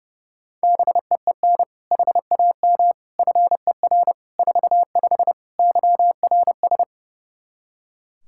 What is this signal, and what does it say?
Morse code, heard as a keyed tone, about 32 wpm, 700 Hz. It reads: BEEN HAM FER 45 YRS
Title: Practice QSO - Exchange 3 - More Info - 30wpm